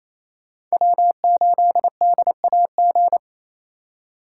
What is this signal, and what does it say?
Morse code, heard as a keyed tone, about 36 wpm, 700 Hz. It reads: W8DAG